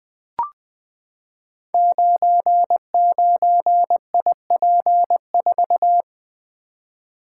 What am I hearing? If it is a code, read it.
Morse code, heard as a keyed tone, about 20 wpm, 700 Hz. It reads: 99IP4
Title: Code Groups - 5 Letter:Numbers - 20wpm